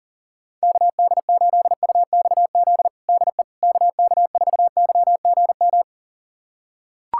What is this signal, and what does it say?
Morse code, heard as a keyed tone, about 40 wpm, 700 Hz. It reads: KD8UXZ DE KK4YGM
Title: USA Callsign DE Callsign - 6 Characters - 40wpm